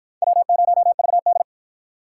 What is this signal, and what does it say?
Morse code, heard as a keyed tone, about 50 wpm, 700 Hz. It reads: W0VD